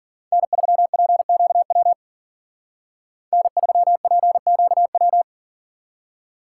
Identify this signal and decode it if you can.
Morse code, heard as a keyed tone, about 52 wpm, 700 Hz. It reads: N3PQW N3PQW